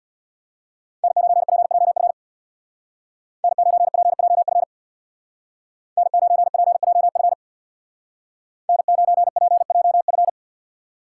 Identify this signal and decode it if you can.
Morse code, heard as a keyed tone, about 61 wpm, 700 Hz. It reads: N9PJF N9PJF N9PJF N9PJF
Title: Speed Racing - USA 1x3 Call Signs 50wpm